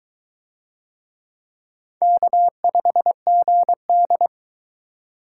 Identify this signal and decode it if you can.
Morse code, heard as a keyed tone, about 22 wpm, 700 Hz. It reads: K5GD